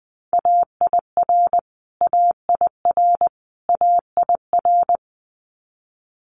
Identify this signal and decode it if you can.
Morse code, heard as a keyed tone, about 20 wpm, 700 Hz. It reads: AIR AIR AIR
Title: Multi-letter Phonemes Repeated 3x 20wpm